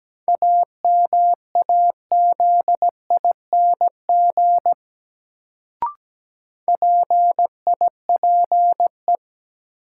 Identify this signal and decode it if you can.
Morse code, heard as a keyed tone, about 17 wpm, 700 Hz. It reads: AMAZING PIPE
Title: Sentences from Top 1500 Words - New Word Drill Alt - 17wpm